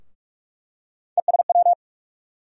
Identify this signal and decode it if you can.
Morse code, heard as a keyed tone, about 41 wpm, 700 Hz. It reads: ESW
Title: USA Call Sign Postfixes - 3 Letters 45wpm